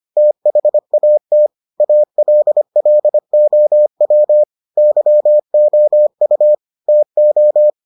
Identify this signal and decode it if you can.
Morse code, heard as a keyed tone, about 25 wpm, 600 Hz. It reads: THAT ALLOW YOU TO